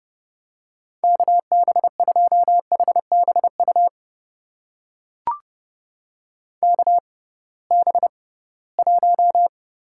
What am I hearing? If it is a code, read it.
Morse code, heard as a keyed tone, about 30 wpm, spaced as for 19 wpm, 700 Hz. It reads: KB2HBU KB1
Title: USA Call Signs - 6 Characters - 6x Character Spacing - 30wpm